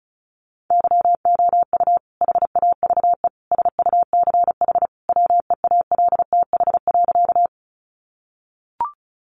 Morse code, heard as a keyed tone, about 34 wpm, 700 Hz. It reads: YOU HAVE SUCH WEALTH.
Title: Sentences from Top 1200 Words 35wpm